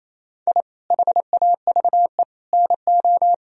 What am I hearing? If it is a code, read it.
Morse code, heard as a keyed tone, about 27 wpm, 700 Hz. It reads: I HAVE NO
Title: Sentences from Top 1000 Words - Review Part 3 - 28wpm